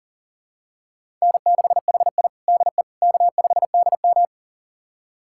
Morse code, heard as a keyed tone, about 40 wpm, 700 Hz. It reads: N6HI DE K5DM
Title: USA Callsign DE Callsign - 4 Characters - 40wpm